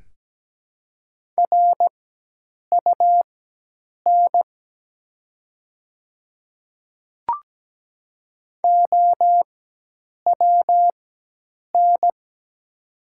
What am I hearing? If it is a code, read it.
Morse code, heard as a keyed tone, about 17 wpm, spaced as for 8 wpm, 700 Hz. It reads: RUN OWN
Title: 4x Character Spacing - Top 300 Words - 17wpm